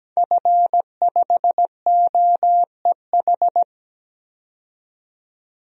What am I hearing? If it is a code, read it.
Morse code, heard as a keyed tone, about 17 wpm, 700 Hz. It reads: F5OEH
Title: DXCC-100 Call Signs - 5 Characters - 17wpm